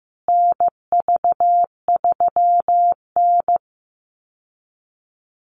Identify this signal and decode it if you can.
Morse code, heard as a keyed tone, about 15 wpm, 700 Hz. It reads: NV3N
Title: Call Signs Repeated 2x - 15wpm